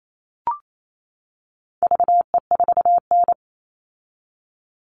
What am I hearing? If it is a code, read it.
Morse code, heard as a keyed tone, about 28 wpm, 700 Hz. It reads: VE4N
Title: All Cut Numbers - 4-Digit 28wpm